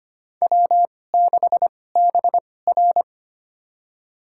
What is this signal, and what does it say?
Morse code, heard as a keyed tone, about 25 wpm, spaced as for 18 wpm, 700 Hz. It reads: W6BR
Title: USA Call Signs - 4 Characters - 2x Character Spacing - 25wpm